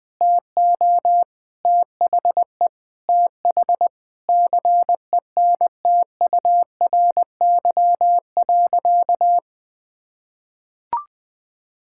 Morse code, heard as a keyed tone, about 21 wpm, 700 Hz. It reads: TO THE TH CENTURY.
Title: Sentences with 6-Word Phrases - Spoken Beginning - 20wpm